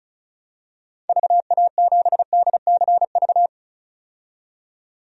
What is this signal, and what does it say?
Morse code, heard as a keyed tone, about 36 wpm, 690 Hz. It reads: VA7DCV
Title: DXCC-100 Call Signs - 6 Characters - 35wpm